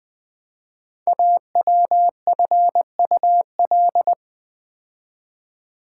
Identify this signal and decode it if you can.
Morse code, heard as a keyed tone, about 20 wpm, 700 Hz. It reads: AWFUL
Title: 5 Letter Words - First 3 Letters Given - 20wpm